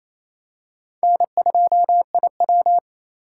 Morse code, heard as a keyed tone, about 28 wpm, 700 Hz. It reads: N2IW
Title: USA Call Signs - 4 Characters - 2x Character Spacing - 28wpm